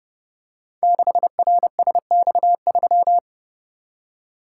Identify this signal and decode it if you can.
Morse code, heard as a keyed tone, about 30 wpm, 700 Hz. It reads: 6RSX3
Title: Code Groups - 5 Letter:Numbers - 30wpm